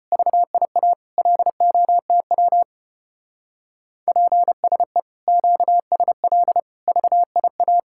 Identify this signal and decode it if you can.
Morse code, heard as a keyed tone, about 29 wpm, 700 Hz. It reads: VIA LOTW PSE QSL VIA
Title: Speed-Racing Common QSO Phrases Encore - 25wpm